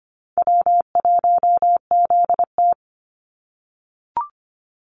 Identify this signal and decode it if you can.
Morse code, heard as a keyed tone, about 25 wpm, 700 Hz. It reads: W1ZT
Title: Easy Call Signs - 25wpm